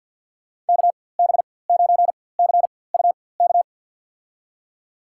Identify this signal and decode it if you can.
Morse code, heard as a keyed tone, about 47 wpm, spaced as for 23 wpm, 700 Hz. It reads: KB9CUK
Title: USA Call Signs - 6 Characters - 4x Character Spacing - 50wpm